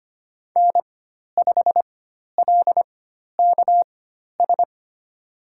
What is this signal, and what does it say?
Morse code, heard as a keyed tone, about 25 wpm, spaced as for 12 wpm, 700 Hz. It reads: N5LKS
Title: USA Call Signs - 5 Characters - 4x Character Spacing - 25wpm